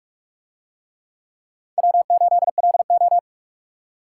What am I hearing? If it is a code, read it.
Morse code, heard as a keyed tone, about 46 wpm, 700 Hz. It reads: W8LO